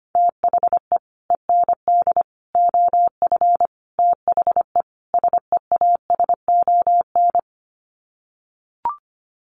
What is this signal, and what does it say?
Morse code, heard as a keyed tone, about 25 wpm, 690 Hz. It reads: THE END OF THE SEASON
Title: Sets of 5 Words Encore 25wpm